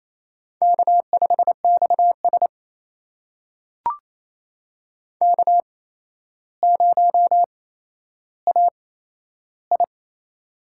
Morse code, heard as a keyed tone, about 27 wpm, spaced as for 12 wpm, 700 Hz. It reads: K5XS K0AI
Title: USA Call Signs - 4 Characters - 8x Character Spacing - 28wpm